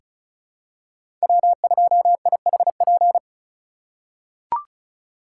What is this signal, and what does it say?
Morse code, heard as a keyed tone, about 36 wpm, 700 Hz. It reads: W2IHP
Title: USA Call Signs - 5 Characters - 4x Character Spacing - 35wpm